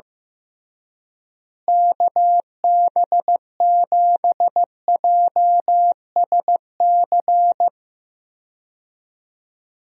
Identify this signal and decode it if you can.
Morse code, heard as a keyed tone, about 16 wpm, 700 Hz. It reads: KB7JSC
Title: USA Call Signs - 6 Characters - Repeated 2x - 15wpm